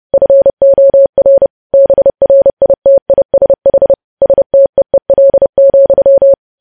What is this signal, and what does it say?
Morse code, heard as a keyed tone, about 30 wpm, 550 Hz. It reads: FOR BRITISH STEEL,
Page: news headlines in morse code at 30 wpm